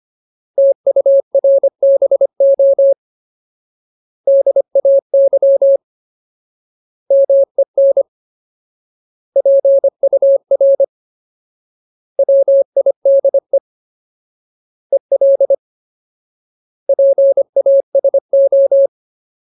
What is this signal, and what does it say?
Morse code, heard as a keyed tone, about 25 wpm, 550 Hz. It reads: TURBO DAY MEN PUR WIDE EL PASO
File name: Ultimate Word List / 25WPM Ultimate